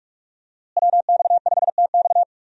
Morse code, heard as a keyed tone, about 46 wpm, 700 Hz. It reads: WX5TX
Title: USA Call Signs - 5 Characters - Repeated 3x - 45wpm